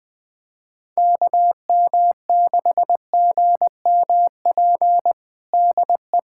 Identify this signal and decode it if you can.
Morse code, heard as a keyed tone, about 21 wpm, 700 Hz. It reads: KM6GMP DE
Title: USA Callsign DE Callsign - 6 Characters - 20wpm